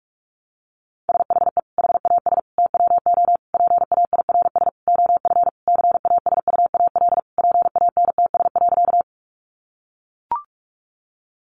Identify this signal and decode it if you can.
Morse code, heard as a keyed tone, about 44 wpm, 700 Hz. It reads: SHE HAS TWO PAIRS OF CASUAL PANTS.
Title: Additional Easy Phrases 45wpm